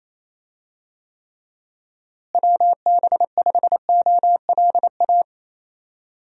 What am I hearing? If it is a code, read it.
Morse code, heard as a keyed tone, about 28 wpm, 700 Hz. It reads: WB5OLA
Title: USA Call Signs - 6 Characters - Repeated 2x - 28wpm